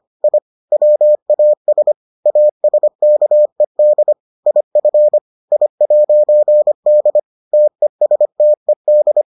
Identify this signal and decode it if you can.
Morse code, heard as a keyed tone, about 25 wpm, 600 Hz. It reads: I WAS ASKED IF I'D TESTED